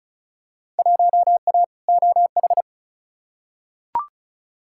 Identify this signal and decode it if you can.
Morse code, heard as a keyed tone, about 34 wpm, 700 Hz. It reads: 1A OH
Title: Field Day Exchanges - 35wpm